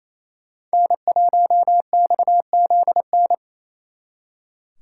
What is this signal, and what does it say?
Morse code, heard as a keyed tone, about 36 wpm, 700 Hz. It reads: N1XZN